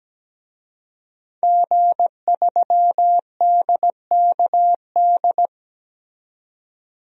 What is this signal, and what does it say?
Morse code, heard as a keyed tone, about 17 wpm, 700 Hz. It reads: G3DKD
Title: DXCC-100 Call Signs - 5 Characters - 17wpm